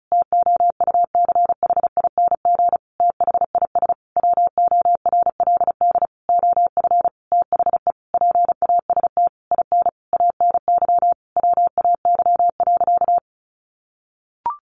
Morse code, heard as a keyed tone, about 34 wpm, 700 Hz. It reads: TOUCHING THIS WORLD OF THE PAST IN ANY WAY.
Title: A Sound Of Thunder 35wpm